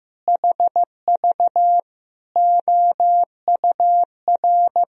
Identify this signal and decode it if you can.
Morse code, heard as a keyed tone, about 15 wpm, 700 Hz. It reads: HV OUR
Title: Sentences from Top 400 Words and Abbreviations 15wpm